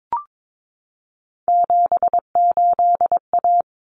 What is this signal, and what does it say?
Morse code, heard as a keyed tone, about 22 wpm, 700 Hz. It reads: 78A
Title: Common Cut Numbers - 3-Digit 22wpm